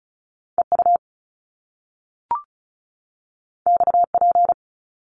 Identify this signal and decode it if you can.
Morse code, heard as a keyed tone, about 36 wpm, 700 Hz. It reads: EU XP
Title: Top Two Letter Combinations in Words 35wpm